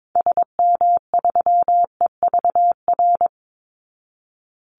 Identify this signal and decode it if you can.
Morse code, heard as a keyed tone, about 22 wpm, 700 Hz. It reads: SM3EVR
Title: Call Signs Repeated 2x - 22wpm